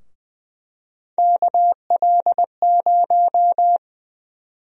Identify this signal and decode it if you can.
Morse code, heard as a keyed tone, about 20 wpm, 700 Hz. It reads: KL0
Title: USA Call Sign Prefixes - 20wpm